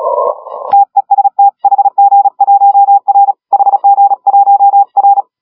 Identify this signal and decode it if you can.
Morse code, heard as a keyed tone, about 36 wpm, 810 Hz. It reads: TEST HG1R HG1R
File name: HG1R CW SSB MIX